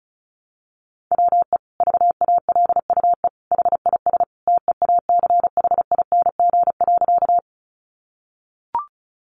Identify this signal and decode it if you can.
Morse code, heard as a keyed tone, about 34 wpm, 700 Hz. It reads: WE VALUE HIS TEACHING.
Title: Sentences from Top 1100 Words 35wpm